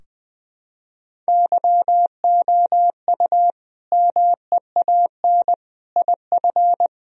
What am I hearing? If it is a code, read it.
Morse code, heard as a keyed tone, about 21 wpm, 700 Hz. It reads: YOU MEAN IF
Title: Sentences from Top 1000 Words - Review Part 4 - 20wpm